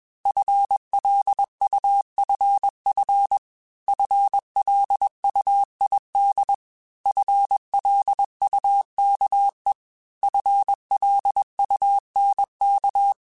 Morse code, heard as a keyed tone, about 21 wpm, 800 Hz. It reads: FLUFF FLUID FLUKE FLUNK
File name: FL Runs / FL Runs mp3s / FL Runs @20